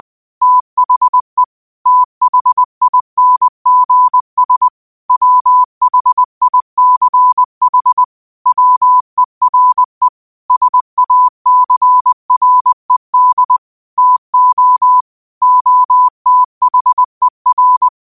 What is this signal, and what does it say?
Morse code, heard as a keyed tone, about 20 wpm, 1000 Hz. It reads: THE THINGS WHICH WERE SACRED TO OTHER